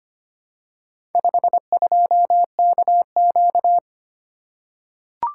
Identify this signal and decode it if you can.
Morse code, heard as a keyed tone, about 25 wpm, 700 Hz. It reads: 52KQ
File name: practice / 25wpm / Four Letter-Number Combinations - 25wpm